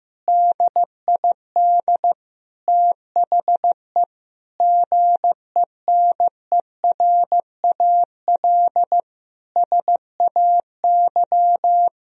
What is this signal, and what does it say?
Morse code, heard as a keyed tone, about 16 wpm, 700 Hz. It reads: DID THE GENERAL SAY
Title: Sentences from Top 1000 Words - Review Part 3 - 15wpm